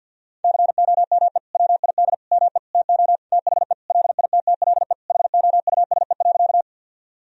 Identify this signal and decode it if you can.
Morse code, heard as a keyed tone, about 50 wpm, 700 Hz. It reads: COME WID ME TO THE LITTLE HOUSE.
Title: Sentences from Top 200 Words and Abbreviations 50wpm